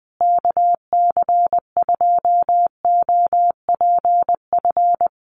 Morse code, heard as a keyed tone, about 20 wpm, 700 Hz. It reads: KC2OPF
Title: USA Call Signs Repeated 2x 20wpm